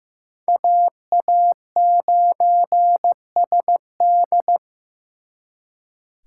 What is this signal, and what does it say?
Morse code, heard as a keyed tone, about 18 wpm, 700 Hz. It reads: AA9SD
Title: Speed Racing - USA 2x2 Call Signs 15wpm